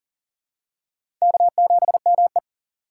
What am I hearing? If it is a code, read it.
Morse code, heard as a keyed tone, about 40 wpm, 700 Hz. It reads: K7ME
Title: USA Call Signs - 4 Characters - 6x Character Spacing - 40wpm